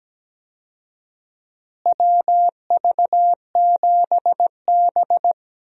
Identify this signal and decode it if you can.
Morse code, heard as a keyed tone, about 17 wpm, 700 Hz. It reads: WV7B
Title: USA Call Signs - 4 Characters - Repeated 2x - 17wpm